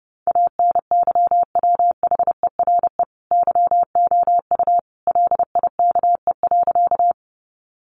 Morse code, heard as a keyed tone, about 30 wpm, 700 Hz. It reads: ANYWHERE YOU LIKE.
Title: Sentences from Top 1400 Words 30wpm